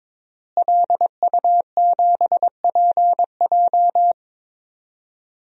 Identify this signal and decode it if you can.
Morse code, heard as a keyed tone, about 21 wpm, 700 Hz. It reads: LU7PJ
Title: DXCC-100 Call Signs - 5 Characters - Repeated 3x - 22wpm